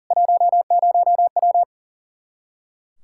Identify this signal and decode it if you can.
Morse code, heard as a keyed tone, about 40 wpm, 700 Hz. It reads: 10W
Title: Practice QSO - Exchange 2 - Info - 40wpm